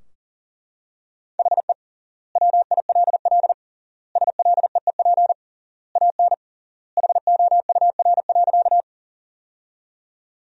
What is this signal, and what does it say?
Morse code, heard as a keyed tone, about 40 wpm, 700 Hz. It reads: HE WILL SLEEP AN HOUR.
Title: 3x Word-Spacing - Sentences from Top 500 Words - 40wpm